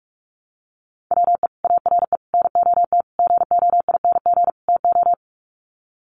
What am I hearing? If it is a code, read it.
Morse code, heard as a keyed tone, about 44 wpm, 700 Hz. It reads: WE ARE NOT GOING TO
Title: Sets of 5 Words 45wpm